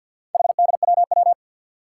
Morse code, heard as a keyed tone, about 51 wpm, 700 Hz. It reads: HDWW
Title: Four Challenging Letter-Number Combinations 50wpm